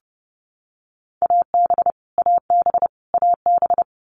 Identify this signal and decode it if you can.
Morse code, heard as a keyed tone, about 30 wpm, 700 Hz. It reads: AB AB AB
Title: Canadian Provinces and Territories - Repeated 3x - 30wpm